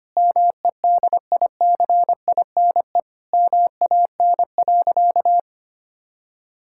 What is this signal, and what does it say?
Morse code, heard as a keyed tone, about 25 wpm, 700 Hz. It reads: MEDICINE MAN.
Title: Sentences from Top 1000 Words - Review Part 6 - 25wpm